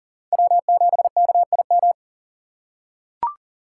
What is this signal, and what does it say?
Morse code, heard as a keyed tone, about 40 wpm, 700 Hz. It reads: W7KIM
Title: USA Call Signs - 5 Characters - 2x Character Spacing - 40wpm